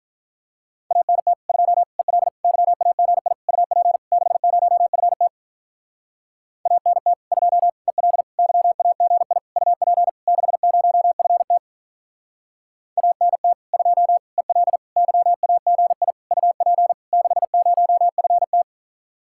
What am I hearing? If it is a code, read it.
Morse code, heard as a keyed tone, about 43 wpm, 700 Hz. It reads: ANT 2 EL YAGI UP 60FT ANT 2 EL YAGI UP 60FT ANT 2 EL YAGI UP 60FT
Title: Speed-Racing Common QSO Phrases - 35wpm